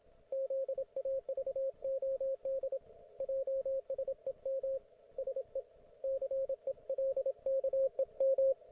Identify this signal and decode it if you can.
Morse code, heard as a keyed tone, about 26 wpm, 540 Hz, with radio noise. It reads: ZAVOD JSEM SE CELKEM